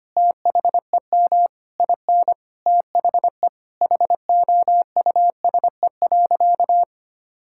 Morse code, heard as a keyed tone, about 25 wpm, 700 Hz. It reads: THEM IN THE HOUSE.
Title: Sentences from Top 1000 Words - Review Part 5 - 25wpm